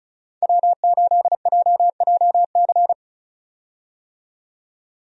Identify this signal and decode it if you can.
Morse code, heard as a keyed tone, about 34 wpm, 700 Hz. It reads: W8JJC
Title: USA Call Signs - 5 Characters - 35wpm